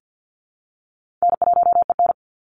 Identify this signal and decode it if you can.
Morse code, heard as a keyed tone, about 55 wpm, 700 Hz. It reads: N1EN
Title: Easy Call Signs Repeated 2x - 50wpm